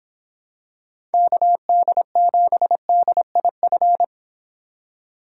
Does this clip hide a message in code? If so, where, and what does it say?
Yes: Morse code from 1.1 to 4.0 s, heard as a keyed tone, about 25 wpm, 700 Hz. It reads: KD7DIF